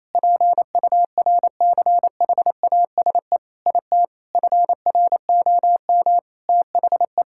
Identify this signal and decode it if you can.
Morse code, heard as a keyed tone, about 28 wpm, 700 Hz. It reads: PURCHASE IT FROM THE
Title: Sentences from Top 1000 Words - Review Part 5 - 28wpm